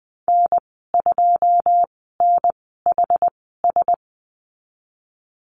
Morse code, heard as a keyed tone, about 19 wpm, spaced as for 14 wpm, 700 Hz. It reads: N2NHS
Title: 179 Lesson 29g - Slash - Call Signs 20wpm